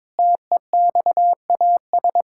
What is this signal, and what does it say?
Morse code, heard as a keyed tone, about 22 wpm, 700 Hz. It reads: TEXAS
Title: Pangram Challenge v2 - 22wpm